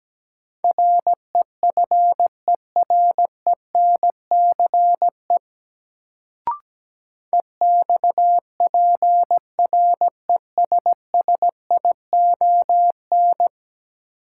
Morse code, heard as a keyed tone, about 17 wpm, 700 Hz. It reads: REFERENCE EXPRESSION
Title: Sentences from Top 1200 Words - New Word Drill - 17wpm